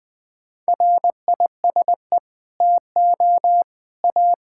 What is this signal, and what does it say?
Morse code, heard as a keyed tone, about 21 wpm, 700 Hz. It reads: RISE TO A
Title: Sentences from Top 1600 Words 20wpm